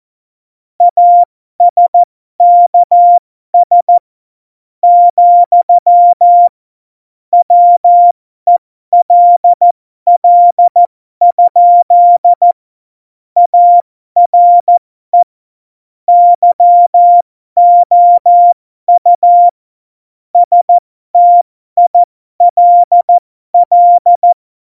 Morse code, heard as a keyed tone, about 14 wpm, spaced as for 12 wpm, 700 Hz. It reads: ASKS, WELL? ARE YOU STILL
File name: Morsekurs / Witze